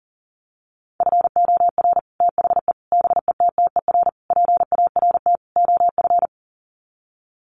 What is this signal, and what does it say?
Morse code, heard as a keyed tone, about 40 wpm, 700 Hz. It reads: FOR THE BETTER PART OF
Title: Sets of 5 Words Encore2 40wpm